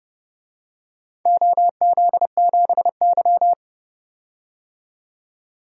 Morse code, heard as a keyed tone, about 30 wpm, 700 Hz. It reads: OZ7Y
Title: DXCC-100 Call Signs - 4 Characters - 30wpm